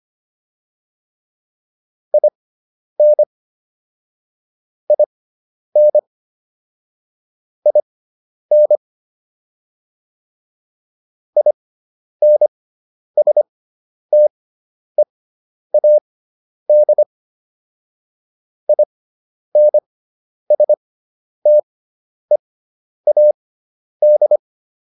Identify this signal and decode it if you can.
Morse code, heard as a keyed tone, about 25 wpm, spaced as for 10 wpm, 600 Hz. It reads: IN IN IN INSTEAD INSTEAD